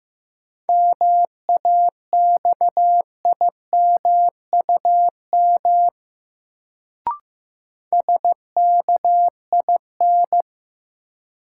Morse code, heard as a keyed tone, about 15 wpm, 700 Hz. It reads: MAXIMUM SKIN
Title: Sentences from Top 1300 Words - New Word Drill Alt - 15wpm